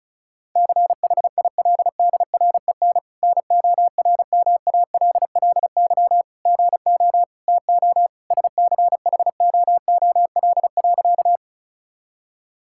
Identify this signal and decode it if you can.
Morse code, heard as a keyed tone, about 34 wpm, 700 Hz. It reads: CHILDREN NORMALLY GO TO SCHOOL.
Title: Sentences from Top 1000 Words - Review Part 6 - 35wpm